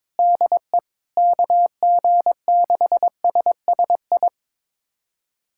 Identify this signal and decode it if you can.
Morse code, heard as a keyed tone, about 22 wpm, 700 Hz. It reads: DE KG6SSI
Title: USA Callsign DE Callsign - 6 Characters - 22wpm